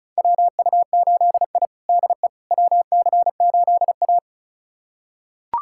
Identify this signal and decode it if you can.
Morse code, heard as a keyed tone, about 34 wpm, 700 Hz. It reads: WU8I DE WC8A
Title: USA Callsign DE Callsign - 4 Characters - 35wpm